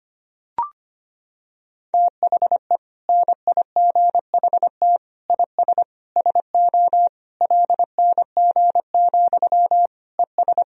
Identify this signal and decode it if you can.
Morse code, heard as a keyed tone, about 25 wpm, 700 Hz. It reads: THE NIGHT IS SO LNG, ES
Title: Sentences from Top 500 Words and Abbreviations 25wpm